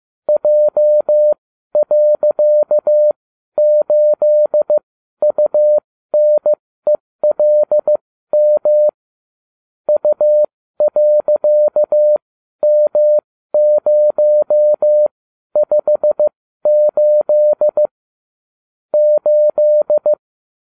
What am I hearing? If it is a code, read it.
Morse code, heard as a keyed tone, about 15 wpm, spaced as for 12 wpm, 600 Hz. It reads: J.8UNELM U.M058 8